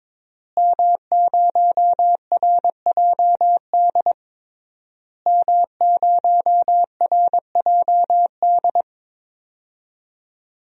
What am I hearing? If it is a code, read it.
Morse code, heard as a keyed tone, about 22 wpm, 700 Hz. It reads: M0RJD M0RJD
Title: DXCC-100 Call Signs - 5 Characters - Repeated 3x - 22wpm